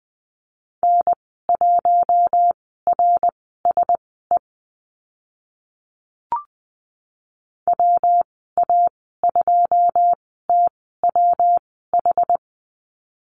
Morse code, heard as a keyed tone, about 20 wpm, spaced as for 15 wpm, 700 Hz. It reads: N1RSE WA2TWH
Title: 122 Lesson 20g - Number 3 - Call Signs 20wpm